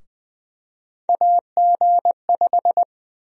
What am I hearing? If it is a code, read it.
Morse code, heard as a keyed tone, about 20 wpm, 700 Hz. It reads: AG5
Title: USA Call Sign Prefixes - 20wpm